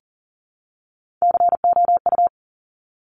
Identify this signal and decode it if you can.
Morse code, heard as a keyed tone, about 41 wpm, 700 Hz. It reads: COU
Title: Top Three Letter Combinations in Words 40wpm